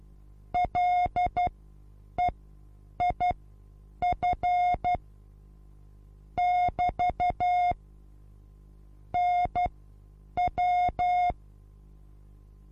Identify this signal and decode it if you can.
Morse code, heard as a keyed tone, about 12 wpm, spaced as for 8 wpm, 720 Hz. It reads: LEIF <BT> NW
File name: Onlinelehrgang / morsen / DJ4UF Morsen23